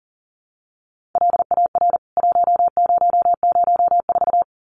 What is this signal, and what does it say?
Morse code, heard as a keyed tone, about 40 wpm, 700 Hz. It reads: LAR 1004
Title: CWT Exchange 40wpm